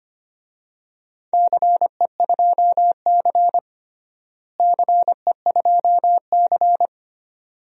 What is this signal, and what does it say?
Morse code, heard as a keyed tone, about 25 wpm, 700 Hz. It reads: CE2C CE2C
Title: DXCC-100 Call Signs - 4 Characters - Repeated 2x - 25wpm